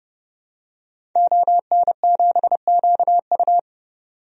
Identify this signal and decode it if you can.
Morse code, heard as a keyed tone, about 30 wpm, 700 Hz. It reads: ON7QU
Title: DXCC-100 Call Signs - 5 Characters - 30wpm